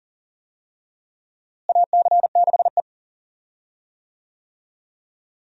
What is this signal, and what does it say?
Morse code, heard as a keyed tone, about 39 wpm, 700 Hz. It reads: AC6E
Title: USA Call Signs - 4 Characters - Repeated 2x - 40wpm